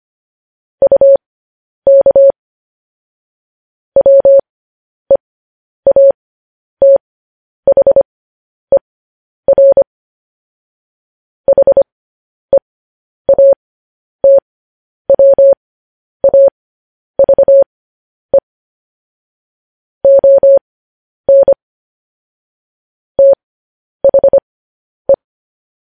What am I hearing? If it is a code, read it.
Morse code, heard as a keyed tone, about 25 wpm, spaced as for 10 wpm, 550 Hz. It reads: UK WEATHER HEATWAVE ON THE
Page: news headlines in morse code at 10 wpm